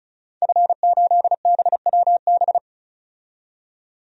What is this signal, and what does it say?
Morse code, heard as a keyed tone, about 34 wpm, 700 Hz. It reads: F8BWB
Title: DXCC-100 Call Signs - 5 Characters - 35wpm